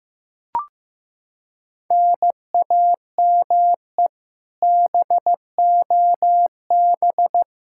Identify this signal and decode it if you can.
Morse code, heard as a keyed tone, about 15 wpm, 700 Hz. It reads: NAME BOB
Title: Practice QSO - Exchange 1 - Beginning - 15wpm